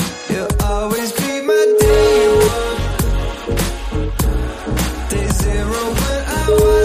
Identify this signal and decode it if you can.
Ringback tone (Europe, 425 Hz), with noise.